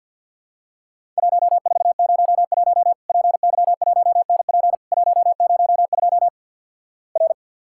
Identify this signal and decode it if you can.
Morse code, heard as a keyed tone, about 50 wpm, 700 Hz. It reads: 1401 PY1NP 102 R
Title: Worked All Europe DX Contest - QTC Exchange - 50wpm